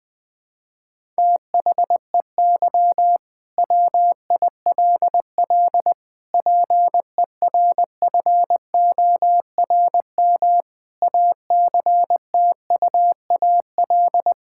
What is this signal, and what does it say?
Morse code, heard as a keyed tone, about 20 wpm, 700 Hz. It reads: THEY WILL PERFORM ACTUAL
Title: Sentences from Top 1000 Words - Review Part 5 - 20wpm